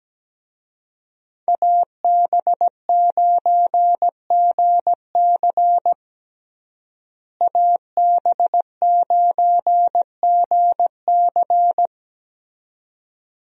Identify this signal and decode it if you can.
Morse code, heard as a keyed tone, about 17 wpm, 700 Hz. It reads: AB9GC AB9GC
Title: USA Call Signs - 5 Characters - Repeated 2x - 17wpm